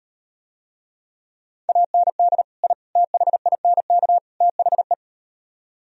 Morse code, heard as a keyed tone, about 38 wpm, 700 Hz. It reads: AND I THINK THE